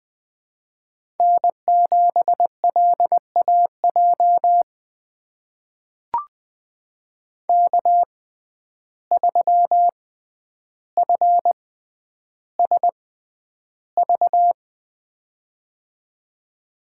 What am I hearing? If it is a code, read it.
Morse code, heard as a keyed tone, about 20 wpm, spaced as for 10 wpm, 700 Hz. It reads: N7LAJ K3FSV
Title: USA Call Signs - 5 Characters - 6x Character Spacing - 20wpm